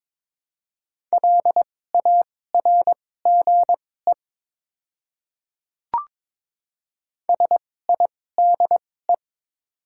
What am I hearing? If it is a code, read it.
Morse code, heard as a keyed tone, about 21 wpm, spaced as for 16 wpm, 700 Hz. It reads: LARGE SIDE
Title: 2x Character Spacing - Top 200 Words - 22wpm